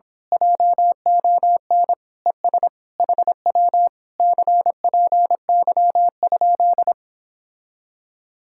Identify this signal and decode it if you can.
Morse code, heard as a keyed tone, about 26 wpm, 700 Hz. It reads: JON ES HW CPY?
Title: Speed-Racing Common QSO Phrases - 22wpm